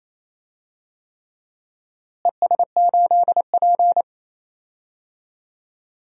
Morse code, heard as a keyed tone, about 28 wpm, 700 Hz. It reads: ES8P
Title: DXCC-100 Call Signs - 4 Characters - Repeated 3x - 28wpm